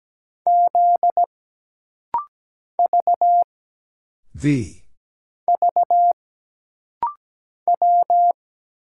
Morse code, heard as a keyed tone, about 17 wpm, 700 Hz. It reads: Z V V W
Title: Single Letters - Rapid-Fire 17wpm